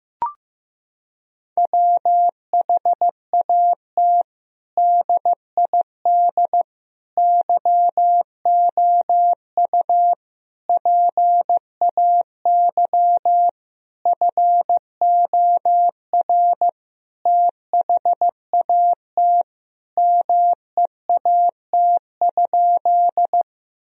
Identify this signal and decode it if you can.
Morse code, heard as a keyed tone, about 15 wpm, 700 Hz. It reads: WHAT DID YOU PAY FOR THAT MEAT?
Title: Sentences from Top 1000 Words - Review Part 3 - 15wpm